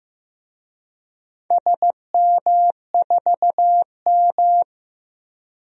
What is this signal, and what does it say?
Morse code, heard as a keyed tone, about 15 wpm, 700 Hz. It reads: SM4M
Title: DXCC-100 Call Signs - 4 Characters - Repeated 3x - 15wpm